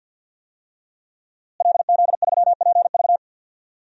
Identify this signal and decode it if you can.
Morse code, heard as a keyed tone, about 52 wpm, 700 Hz. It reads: LZ3PV
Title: DXCC-100 Call Signs - 5 Characters - Repeated 2x - 50wpm